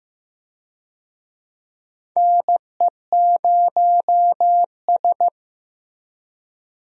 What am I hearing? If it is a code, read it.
Morse code, heard as a keyed tone, about 15 wpm, 700 Hz. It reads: NE0S
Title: USA Call Signs - 4 Characters - Repeated 2x - 15wpm